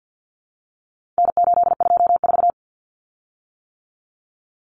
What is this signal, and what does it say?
Morse code, heard as a keyed tone, about 50 wpm, 700 Hz. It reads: N824
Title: Common Cut Numbers - 4-Digit 50wpm